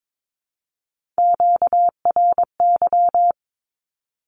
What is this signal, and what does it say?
Morse code, heard as a keyed tone, about 22 wpm, 700 Hz. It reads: QRY
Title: Q-Codes 22wpm